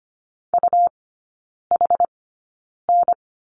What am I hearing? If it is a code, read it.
Morse code, heard as a keyed tone, about 25 wpm, spaced as for 9 wpm, 700 Hz. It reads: UHN
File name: Lesson 17e - Letter u - Three Characters 25wpm